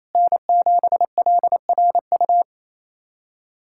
Morse code, heard as a keyed tone, about 28 wpm, 700 Hz. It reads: N7LRU
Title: USA Callsign DE Callsign 28wpm